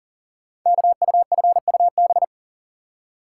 Morse code, heard as a keyed tone, about 40 wpm, 700 Hz. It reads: KUFUB
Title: Code Groups - 5 Letters - 40wpm